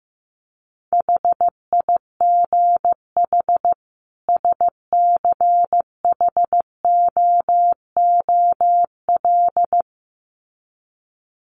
Morse code, heard as a keyed tone, about 15 wpm, 700 Hz. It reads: HIGH SCHOOL
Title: Sets of 2 Words 15wpm